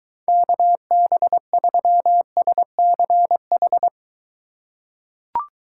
Morse code, heard as a keyed tone, about 23 wpm, 700 Hz. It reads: KB3SCH